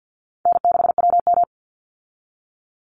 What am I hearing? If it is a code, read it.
Morse code, heard as a keyed tone, about 49 wpm, 700 Hz. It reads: N6WM
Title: Call Signs Repeated 3x - 50wpm